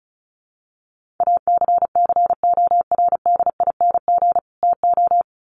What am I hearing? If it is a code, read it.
Morse code, heard as a keyed tone, about 34 wpm, 700 Hz. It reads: ACCORDING TO
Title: Sets of 2 Words 35wpm